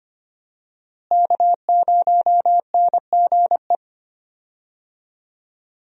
Morse code, heard as a keyed tone, about 25 wpm, 700 Hz. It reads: K0NGE
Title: USA Call Signs - 5 Characters - Repeated 2x - 25wpm